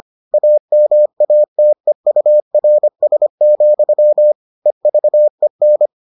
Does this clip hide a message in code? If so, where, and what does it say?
Yes: Morse code from 0.3 to 5.9 s, heard as a keyed tone, about 25 wpm, 600 Hz. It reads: AMATEURS, EVEN